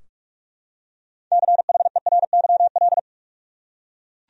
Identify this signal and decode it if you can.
Morse code, heard as a keyed tone, about 45 wpm, 700 Hz. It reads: CHERYL
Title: USA Common Names - 45wpm